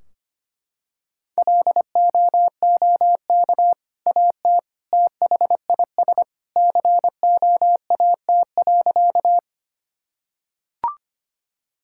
Morse code, heard as a keyed tone, about 25 wpm, 700 Hz. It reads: LOOK AT THIS COAT.
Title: Sentences from Top 1600 Words 25wpm